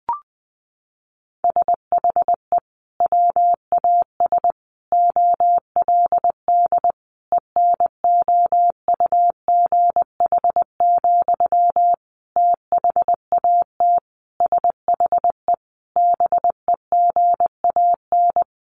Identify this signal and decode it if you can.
Morse code, heard as a keyed tone, about 20 wpm, 700 Hz. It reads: SHE WAS OLD ENOUGH, THAT SHE BEGAN
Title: Sentences from Top 800 Words 20wpm